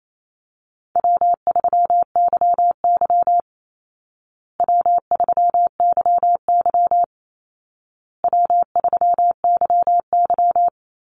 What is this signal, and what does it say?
Morse code, heard as a keyed tone, about 28 wpm, 700 Hz. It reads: W3YY W3YY W3YY
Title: Easy Call Signs Repeated 3x - 28wpm